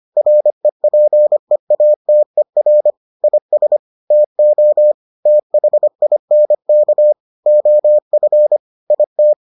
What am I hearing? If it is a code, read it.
Morse code, heard as a keyed tone, about 25 wpm, 600 Hz. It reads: REPEATER IS TO THINK OF IT